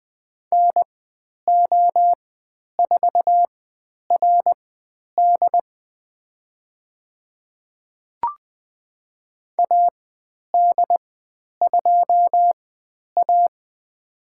Morse code, heard as a keyed tone, about 20 wpm, spaced as for 10 wpm, 700 Hz. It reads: NO4RD AD2A
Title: 095 Lesson 16g - Letter c - Call Signs 20wpm